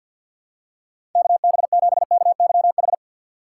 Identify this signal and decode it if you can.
Morse code, heard as a keyed tone, about 47 wpm, 700 Hz. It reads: KB7KYH
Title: USA Call Signs - 6 Characters - 6x Character Spacing - 50wpm